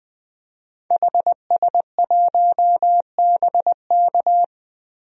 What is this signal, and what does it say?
Morse code, heard as a keyed tone, about 21 wpm, 700 Hz. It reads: HS1BK
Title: DXCC-100 Call Signs - 5 Characters - 20wpm